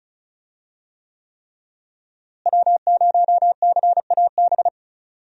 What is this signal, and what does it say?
Morse code, heard as a keyed tone, about 34 wpm, 700 Hz. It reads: W0CAB
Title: USA Call Signs - 5 Characters - Repeated 3x - 35wpm